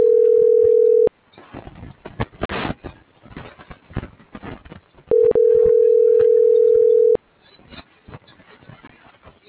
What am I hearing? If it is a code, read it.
Ringback tone (North America), 440 + 480 Hz.
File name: CENTRAL TERRESTRE / out claudiacruz - .65795